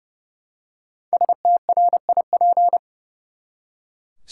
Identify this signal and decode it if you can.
Morse code, heard as a keyed tone, about 30 wpm, 700 Hz. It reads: STRIP
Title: Five Letter Words - 30wpm